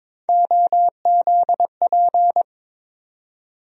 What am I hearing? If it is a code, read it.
Morse code, heard as a keyed tone, about 22 wpm, 700 Hz. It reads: OZP
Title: USA Call Sign Postfixes - 3 Letters 22wpm